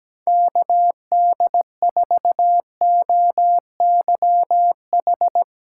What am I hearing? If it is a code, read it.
Morse code, heard as a keyed tone, about 17 wpm, 700 Hz. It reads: KD4OYH
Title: USA Callsign DE Callsign - 6 Characters - 17wpm